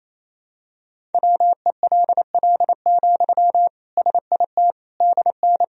Morse code, heard as a keyed tone, about 28 wpm, 700 Hz. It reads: WELL, SIT DN
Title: Sentences from Top 500 Words and Abbreviations 28wpm